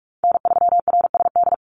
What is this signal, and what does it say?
Morse code, heard as a keyed tone, about 44 wpm, 700 Hz. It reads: N3RSD
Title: Call Signs Repeated 3x - 45wpm